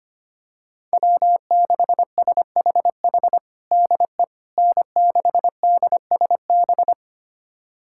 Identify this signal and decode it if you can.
Morse code, heard as a keyed tone, about 25 wpm, 700 Hz. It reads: W6SHH DE N6DSB
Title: USA Callsign DE Callsign - 5 Characters - 25wpm